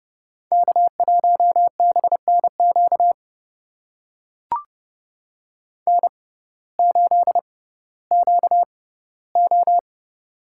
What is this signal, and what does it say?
Morse code, heard as a keyed tone, about 32 wpm, spaced as for 17 wpm, 700 Hz. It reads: K1BNQ N8QO
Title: USA Call Signs - 6x Character Spacing - 30wpm